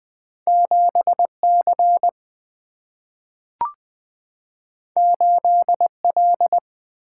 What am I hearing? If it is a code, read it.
Morse code, heard as a keyed tone, about 20 wpm, 700 Hz. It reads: 7 C 8 L
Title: Single-Letter Single-Number 20wpm